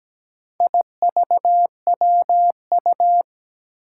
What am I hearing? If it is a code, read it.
Morse code, heard as a keyed tone, about 17 wpm, 700 Hz. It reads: IVWU
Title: Four Challenging Letter-Number Combinations 17wpm